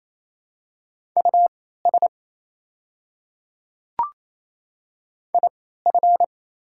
Morse code, heard as a keyed tone, about 28 wpm, spaced as for 16 wpm, 700 Hz. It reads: US IF
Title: 3x Character Spacing - Top 200 Words - 28wpm